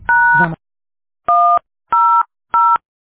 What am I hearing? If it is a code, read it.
DTMF, heard as telephone touch tones, tones of about 320 ms. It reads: # 100